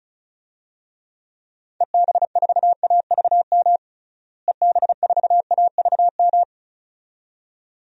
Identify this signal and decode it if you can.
Morse code, heard as a keyed tone, about 36 wpm, 700 Hz. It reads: EB4AVM EB4AVM
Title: DXCC-100 Call Signs - 6 Characters - Repeated 2x - 35wpm